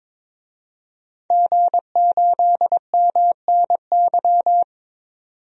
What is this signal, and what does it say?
Morse code, heard as a keyed tone, about 22 wpm, 700 Hz. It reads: G8MNY
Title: DXCC-100 Call Signs - 5 Characters - 22wpm